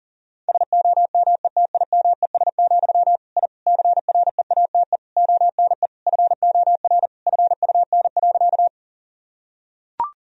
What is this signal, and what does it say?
Morse code, heard as a keyed tone, about 40 wpm, 700 Hz. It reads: SOMETIMES, I CREATE ONE FOR FUN.
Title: Sentences from Top 1000 Words - Review Part 3 - 40wpm